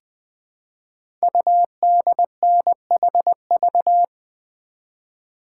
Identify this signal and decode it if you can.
Morse code, heard as a keyed tone, about 20 wpm, 700 Hz. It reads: UDNHV
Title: Code Groups - 5 Letters - 20wpm